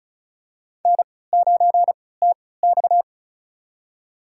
Morse code, heard as a keyed tone, about 34 wpm, spaced as for 20 wpm, 700 Hz. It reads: N9TX
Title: USA Call Signs - 3x Character Spacing - 35wpm